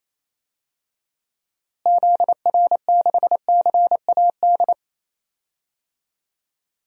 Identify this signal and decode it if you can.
Morse code, heard as a keyed tone, about 28 wpm, 700 Hz. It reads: ZR6CAD
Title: DXCC-100 Call Signs - 6 Characters - Repeated 2x - 28wpm